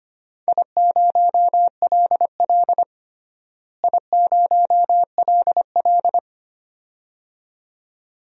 Morse code, heard as a keyed tone, about 25 wpm, 700 Hz. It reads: I0LL I0LL
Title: DXCC-100 Call Signs - 4 Characters - Repeated 2x - 25wpm